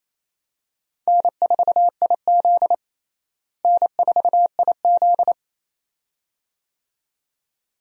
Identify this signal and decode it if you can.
Morse code, heard as a keyed tone, about 28 wpm, 700 Hz. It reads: N4IZ N4IZ
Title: USA Call Signs - 4 Characters - Repeated 2x - 28wpm